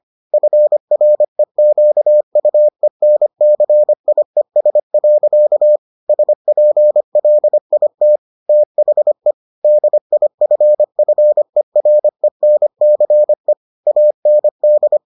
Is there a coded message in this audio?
Yes: Morse code from 0.3 to 15.0 s, heard as a keyed tone, about 25 wpm, 600 Hz. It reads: FREQUENCIES. SPLIT THE DIFFERENCE AND